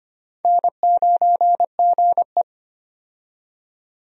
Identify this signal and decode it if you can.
Morse code, heard as a keyed tone, about 25 wpm, 700 Hz. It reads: N9GE
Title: USA Call Signs - 4 Characters - 25wpm